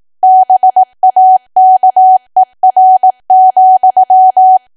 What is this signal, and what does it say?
Morse code, heard as a keyed tone, about 18 wpm, 750 Hz. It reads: BAKER,
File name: Archive / 18 WPM / 18WPM